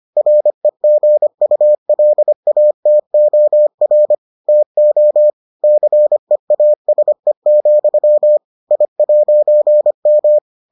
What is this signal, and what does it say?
Morse code, heard as a keyed tone, about 25 wpm, 600 Hz. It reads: REGULATOR TO CEASE, I'M